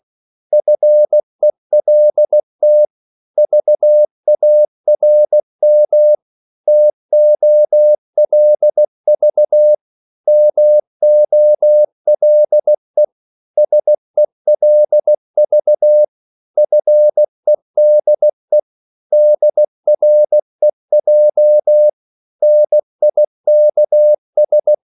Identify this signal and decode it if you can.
Morse code, heard as a keyed tone, about 16 wpm, 600 Hz. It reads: FELT VARM TOLV MOLE SELV FEDE DREJ NIKS
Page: Korte danske ord 16wpm